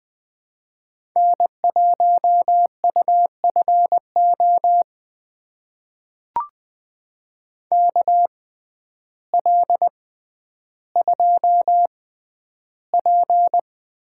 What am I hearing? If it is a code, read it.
Morse code, heard as a keyed tone, about 20 wpm, spaced as for 11 wpm, 700 Hz. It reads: N1UFO KL2P
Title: USA Call Signs - 5 Characters - 6x Character Spacing - 20wpm